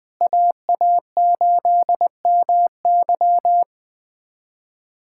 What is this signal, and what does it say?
Morse code, heard as a keyed tone, about 20 wpm, 700 Hz. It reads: AA8MY
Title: USA Callsign DE Callsign - 5 Characters - 20wpm